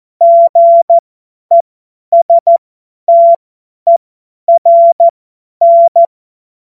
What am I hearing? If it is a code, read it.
Morse code, heard as a keyed tone, about 14 wpm, spaced as for 10 wpm, 680 Hz. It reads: GESTERN
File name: Morsekurs / Witze / Witz 20. Januar 2026 50 F70